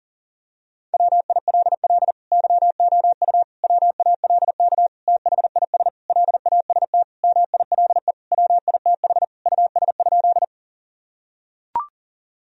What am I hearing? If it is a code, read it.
Morse code, heard as a keyed tone, about 40 wpm, 700 Hz. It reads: WILL YOU WALK THIS LAST MILE WITH US?
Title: Sentences from Top 1000 Words - Review Part 1 - 40wpm